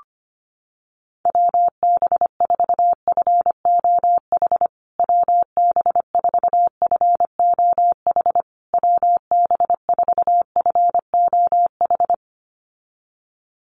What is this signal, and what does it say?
Morse code, heard as a keyed tone, about 25 wpm, 700 Hz. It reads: WB4FOH WB4FOH WB4FOH
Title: USA Call Signs Repeated 3x 25wpm